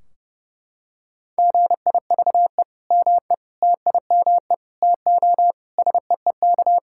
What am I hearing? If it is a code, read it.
Morse code, heard as a keyed tone, about 30 wpm, 700 Hz. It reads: GIVE ME TIME TO SEEK
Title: Sentences from Top 1600 Words 30wpm